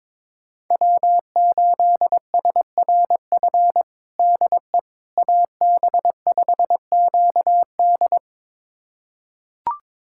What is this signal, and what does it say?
Morse code, heard as a keyed tone, about 22 wpm, 700 Hz. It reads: W8SRF DE AB5QD
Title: USA Callsign DE Callsign - 5 Characters - 22wpm